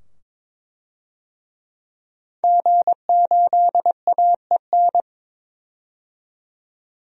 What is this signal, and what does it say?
Morse code, heard as a keyed tone, about 21 wpm, 700 Hz. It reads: G8AEN
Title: DXCC-100 Call Signs - 5 Characters - 22wpm